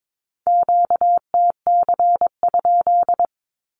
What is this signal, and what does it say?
Morse code, heard as a keyed tone, about 22 wpm, 700 Hz. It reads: QTC?
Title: Q-Codes with Spoken Meaning - 22wpm